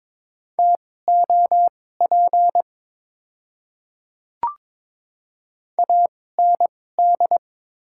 Morse code, heard as a keyed tone, about 22 wpm, spaced as for 16 wpm, 690 Hz. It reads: TOP AND
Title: 2x Character Spacing - Top 500 Words - 22wpm